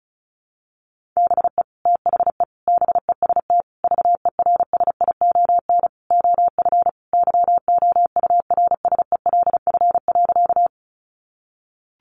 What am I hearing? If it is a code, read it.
Morse code, heard as a keyed tone, about 34 wpm, 700 Hz. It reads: BE THE BEST VERSION OF YOURSELF.
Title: Sentences from Top 1400 Words 35wpm